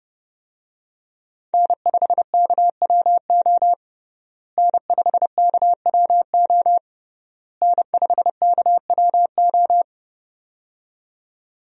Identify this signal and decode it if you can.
Morse code, heard as a keyed tone, about 30 wpm, 700 Hz. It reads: N5KWO N5KWO N5KWO
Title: USA Call Signs - 5 Characters - Repeated 3x - 30wpm